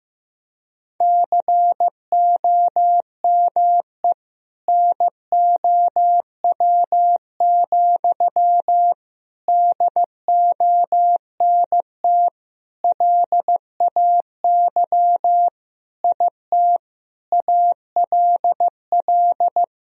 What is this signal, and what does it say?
Morse code, heard as a keyed tone, about 14 wpm, 700 Hz. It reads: COME NOW, DONT LAY IT ALL
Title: Sentences from Top 1000 Words - Review Part 2 - 15wpm